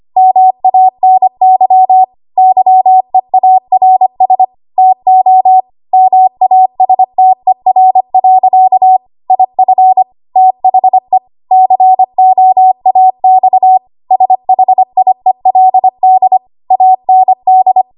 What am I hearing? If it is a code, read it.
Morse code, heard as a keyed tone, about 25 wpm, 750 Hz. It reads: MANY YEARS TO MASTER. IF THE COAX SHIELD AND
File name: Archive / 25 WPM / 25WPM